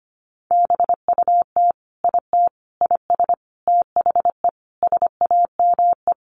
Morse code, heard as a keyed tone, about 25 wpm, 700 Hz. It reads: BUT IT IS THE SAME
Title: Sentences from Top 300 Words 25wpm